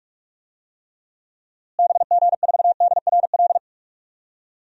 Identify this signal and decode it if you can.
Morse code, heard as a keyed tone, about 43 wpm, 700 Hz. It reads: BG4DRL
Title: DXCC-100 Call Signs - 6 Characters - Repeated 3x - 45wpm